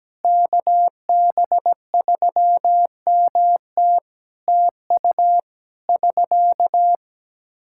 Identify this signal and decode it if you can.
Morse code, heard as a keyed tone, about 17 wpm, 700 Hz. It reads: KB3MT TU <SK>
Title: Practice QSO - Exchange 4 - Ending - 17wpm